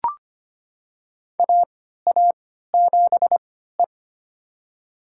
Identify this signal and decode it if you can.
Morse code, heard as a keyed tone, about 25 wpm, spaced as for 14 wpm, 700 Hz. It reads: AA7E
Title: USA Call Signs - 4 Characters - 3x Character Spacing - 25wpm